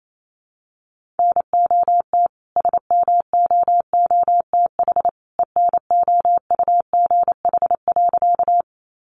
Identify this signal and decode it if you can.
Morse code, heard as a keyed tone, about 28 wpm, 700 Hz. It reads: NOT SMOOTH ENOUGH.
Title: Sentences from Top 1400 Words 28wpm